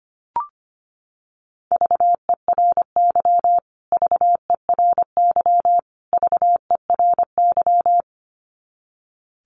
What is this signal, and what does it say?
Morse code, heard as a keyed tone, about 25 wpm, 700 Hz. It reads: VERY VERY VERY
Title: Top 300 Words Repeated 25wpm